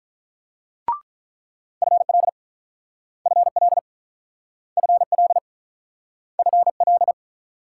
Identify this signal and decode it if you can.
Morse code, heard as a keyed tone, about 48 wpm, 700 Hz. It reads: FL FL FL FL